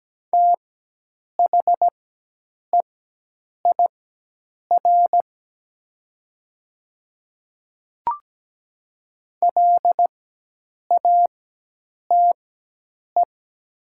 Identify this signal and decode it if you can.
Morse code, heard as a keyed tone, about 17 wpm, spaced as for 8 wpm, 700 Hz. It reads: THEIR LATE
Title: 4x Character Spacing - Top 300 Words - 17wpm